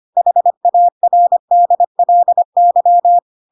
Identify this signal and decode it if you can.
Morse code, heard as a keyed tone, about 25 wpm, 700 Hz. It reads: HARDLY
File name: Pinocchio / Pinocchio ch1 25 25 ik1hge